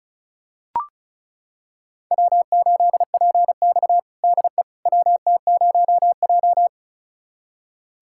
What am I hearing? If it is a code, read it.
Morse code, heard as a keyed tone, about 36 wpm, 700 Hz. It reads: W8PX DE WT0J
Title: USA Callsign DE Callsign - 4 Characters - 35wpm